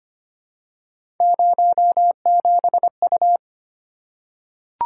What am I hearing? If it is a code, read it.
Morse code, heard as a keyed tone, about 25 wpm, 700 Hz. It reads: 07U
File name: Three Letter-Number Combinations - 25wpm